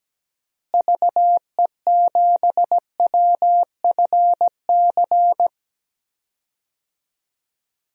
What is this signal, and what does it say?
Morse code, heard as a keyed tone, about 17 wpm, 700 Hz. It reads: VE7WFC
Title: DXCC-100 Call Signs - 6 Characters - Repeated 3x - 17wpm